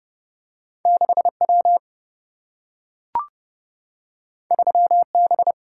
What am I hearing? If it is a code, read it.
Morse code, heard as a keyed tone, about 31 wpm, 700 Hz. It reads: 6W 3B
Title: Single-Letter Single-Number 30wpm